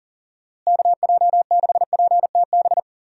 Morse code, heard as a keyed tone, about 40 wpm, 700 Hz. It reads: KJ6PTB
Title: USA Call Signs - 6 Characters - Repeated 3x - 40wpm